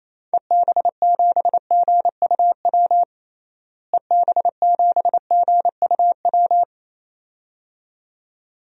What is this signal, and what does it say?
Morse code, heard as a keyed tone, about 28 wpm, 700 Hz. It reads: EB7GUW EB7GUW
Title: DXCC-100 Call Signs - 6 Characters - Repeated 2x - 28wpm